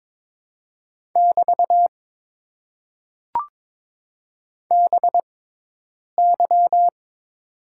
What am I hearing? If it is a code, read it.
Morse code, heard as a keyed tone, about 22 wpm, spaced as for 8 wpm, 700 Hz. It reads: <BT> BY
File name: Top 90 QSO Elements - 6x Character Spacing - 22wpm